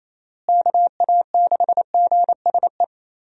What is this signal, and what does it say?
Morse code, heard as a keyed tone, about 28 wpm, 700 Hz. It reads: KA6GSE
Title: USA Call Signs - 6 Characters - Repeated 3x - 28wpm